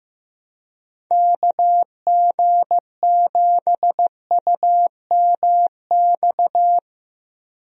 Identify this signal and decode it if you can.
Morse code, heard as a keyed tone, about 14 wpm, 700 Hz. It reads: KG7UMX
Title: USA Call Signs - 6 Characters - Repeated 3x - 15wpm